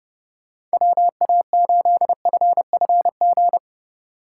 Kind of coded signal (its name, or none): Morse code